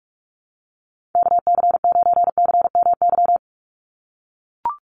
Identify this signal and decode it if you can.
Morse code, heard as a keyed tone, about 44 wpm, 700 Hz. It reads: KC9CMY
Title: USA Call Signs 45wpm